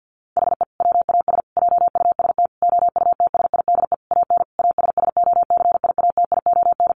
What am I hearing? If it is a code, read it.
Morse code, heard as a keyed tone, about 50 wpm, 700 Hz. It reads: HE WAS JUST OUTSIDE AN ASSOCIATION
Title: Sentences from Top 1100 Words 50wpm